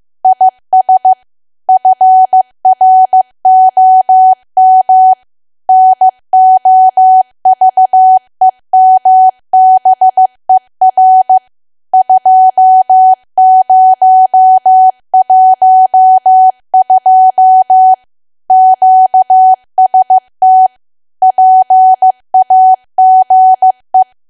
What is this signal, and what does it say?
Morse code, heard as a keyed tone, about 15 wpm, 750 Hz. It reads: IS FROM NOVEMBER 2012 QST PAGE
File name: cwa / W1AW-Level2 / 15WPM